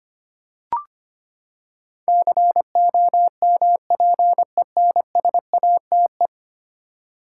Morse code, heard as a keyed tone, about 25 wpm, 700 Hz. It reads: COMPENSATE
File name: Ten Letter Words - 25wpm